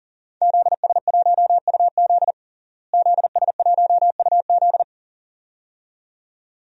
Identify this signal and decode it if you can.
Morse code, heard as a keyed tone, about 40 wpm, 700 Hz. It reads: ZS1UZ ZS1UZ
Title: DXCC-100 Call Signs - 5 Characters - Repeated 2x - 40wpm